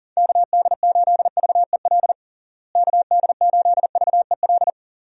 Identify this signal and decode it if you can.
Morse code, heard as a keyed tone, about 40 wpm, 700 Hz. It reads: KD8VEL KD8VEL
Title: USA Call Signs - 6 Characters - Repeated 3x - 40wpm